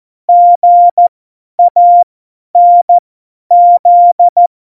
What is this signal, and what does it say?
Morse code, heard as a keyed tone, about 14 wpm, spaced as for 10 wpm, 700 Hz. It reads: GANZ
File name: Morsekurs / Witze / Witz 27 Juni 17, Tempo 50 F70